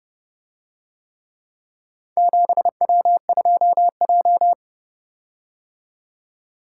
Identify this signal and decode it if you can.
Morse code, heard as a keyed tone, about 29 wpm, 700 Hz. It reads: 7W2J